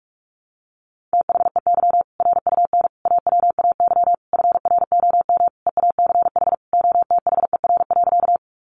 Morse code, heard as a keyed tone, about 44 wpm, 700 Hz. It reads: THEY RUN AWAY FROM EACH OTHER.
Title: Easy Phrases 45wpm